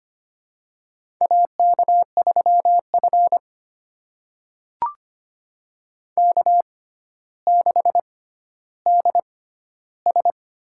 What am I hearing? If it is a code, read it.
Morse code, heard as a keyed tone, about 25 wpm, spaced as for 13 wpm, 700 Hz. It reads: AK3F K6DS
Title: USA Call Signs - 4 Characters - 6x Character Spacing - 25wpm